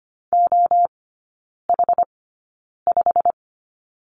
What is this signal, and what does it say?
Morse code, heard as a keyed tone, about 25 wpm, spaced as for 9 wpm, 690 Hz. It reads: OH5
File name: Lesson 15e - Number 5 - Three Characters 25wpm